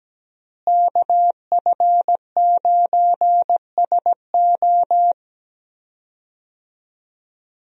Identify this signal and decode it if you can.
Morse code, heard as a keyed tone, about 17 wpm, 700 Hz. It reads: KF9SO
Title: USA Call Signs - 5 Characters - Repeated 3x - 17wpm